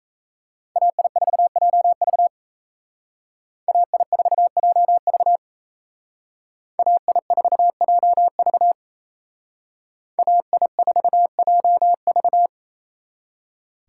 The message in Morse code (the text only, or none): AI4JV AI4JV AI4JV AI4JV